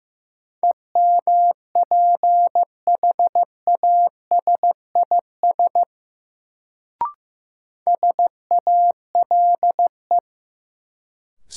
Morse code, heard as a keyed tone, about 15 wpm, 700 Hz. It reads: EMPHASIS SALE
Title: Sentences from Top 1400 Words - New Word Drill Alt - 15wpm